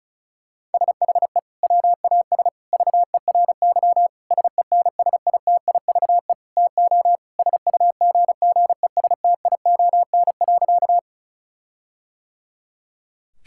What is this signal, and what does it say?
Morse code, heard as a keyed tone, about 34 wpm, 700 Hz. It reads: SHE WAS VERY SENSITIVE TO SUGGESTION.
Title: Sentences from Top 1600 Words 35wpm